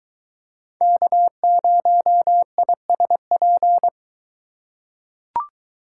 Morse code, heard as a keyed tone, about 21 wpm, 700 Hz. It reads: K0ISP